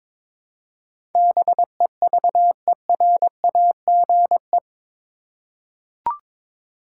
Morse code, heard as a keyed tone, about 22 wpm, 700 Hz. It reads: BEVERAGE